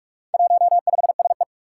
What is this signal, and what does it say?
Morse code, heard as a keyed tone, about 44 wpm, 700 Hz. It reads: 15SE
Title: Four Challenging Letter-Number Combinations 45wpm